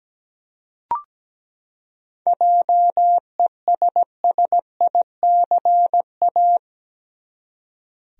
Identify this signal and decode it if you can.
Morse code, heard as a keyed tone, about 17 wpm, 700 Hz. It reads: JESSICA
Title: USA Common Names - 17wpm